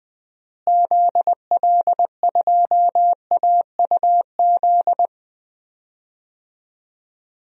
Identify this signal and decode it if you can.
Morse code, heard as a keyed tone, about 21 wpm, 700 Hz. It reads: ZL2AUZ
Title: DXCC-100 Call Signs - 6 Characters - Repeated 3x - 20wpm